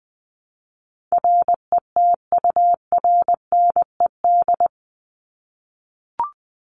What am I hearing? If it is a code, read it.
Morse code, heard as a keyed tone, about 20 wpm, 700 Hz. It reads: RETURNED
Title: ED Suffix - Top 100 Words with Base Words 20wpm